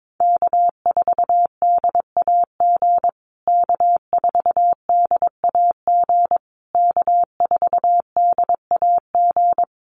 Morse code, heard as a keyed tone, about 22 wpm, 700 Hz. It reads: K4DAG K4DAG K4DAG
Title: USA Call Signs Repeated 3x 22wpm